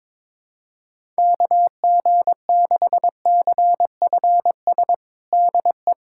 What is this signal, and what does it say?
Morse code, heard as a keyed tone, about 22 wpm, 700 Hz. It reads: KG6CFS DE
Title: USA Callsign DE Callsign - 6 Characters - 22wpm